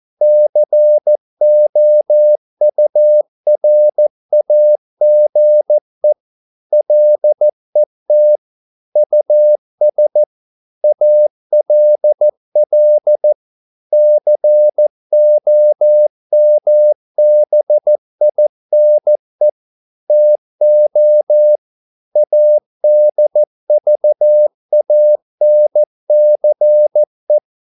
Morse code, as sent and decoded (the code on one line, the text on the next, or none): -.-. --- ..- .-. .- --. / .-.. - / ..- ... / .- .-.. .-.. / -.-. --- -- -... .. -. / - --- / .- -.. ...- .- -. -.-. .
COURAGE LET US ALL COMBINE TO ADVANCE